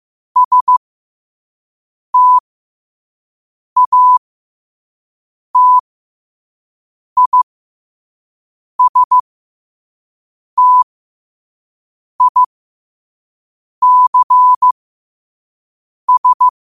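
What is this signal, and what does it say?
Morse code, heard as a keyed tone, about 15 wpm, spaced as for 5 wpm, 1000 Hz. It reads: STATISTICS
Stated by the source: Morse code at 5 wpm.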